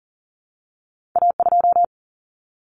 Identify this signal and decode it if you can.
Morse code, heard as a keyed tone, about 38 wpm, 700 Hz. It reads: A2
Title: Common Cut Numbers - 2-Digit 40wpm